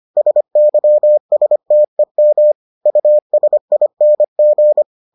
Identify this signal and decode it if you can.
Morse code, heard as a keyed tone, about 25 wpm, 600 Hz. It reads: SYSTEM USING